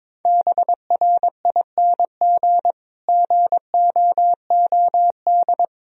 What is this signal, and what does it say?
Morse code, heard as a keyed tone, about 21 wpm, 700 Hz. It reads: BRING GOOD
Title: Sentences from Top 1000 Words - Review Part 4 - 22wpm